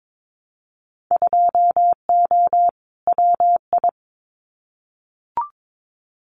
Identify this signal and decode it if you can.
Morse code, heard as a keyed tone, about 22 wpm, 700 Hz. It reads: 2O WI
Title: Winter Field Day Exchanges 22wpm